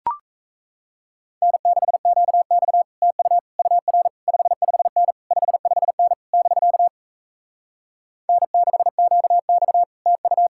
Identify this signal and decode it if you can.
Morse code, heard as a keyed tone, about 38 wpm, 700 Hz. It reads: N6QX TU UR 55N 55N <BK> N6QX TU